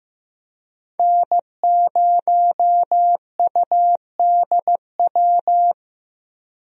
Morse code, heard as a keyed tone, about 15 wpm, 700 Hz. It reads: N0UDW
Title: USA Call Signs - 5 Characters - 15wpm